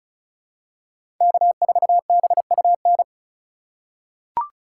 Morse code, heard as a keyed tone, about 36 wpm, 700 Hz. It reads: K4BUN
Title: USA Call Signs - 5 Characters - 8x Character Spacing - 35wpm